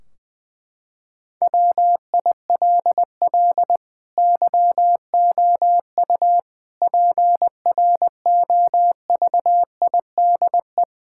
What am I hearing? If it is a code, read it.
Morse code, heard as a keyed tone, about 20 wpm, 700 Hz. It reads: WILL YOU PROVIDE
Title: Sentences from Top 1600 Words 20wpm